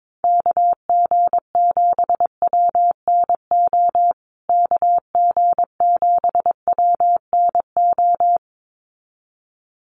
Morse code, heard as a keyed tone, about 22 wpm, 700 Hz. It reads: KG7WNO KG7WNO
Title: USA Call Signs Repeated 2x 22wpm